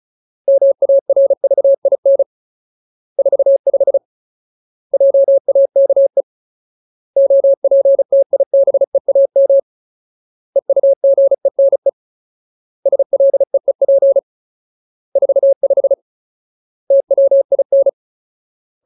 Morse code, heard as a keyed tone, about 34 wpm, 550 Hz. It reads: MARVIN 45 JAKE OPTIBEAM EUGENE SLEEP 45 TWIN
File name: Ultimate Word List / 35WPM Ultimate